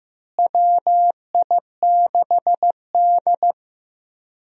Morse code, heard as a keyed tone, about 16 wpm, 700 Hz. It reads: WI6D
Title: USA Call Signs - 4 Characters - Repeated 3x - 15wpm